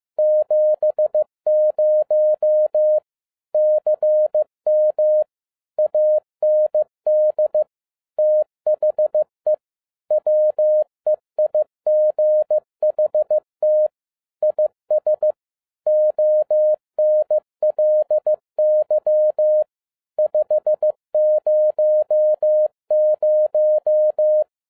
Morse code, heard as a keyed tone, about 15 wpm, 610 Hz. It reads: 70 CM AND THE WEIGHT IS ONLY 500
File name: 2012-01-mp3 / n07-ARA JR0DIL-15wpm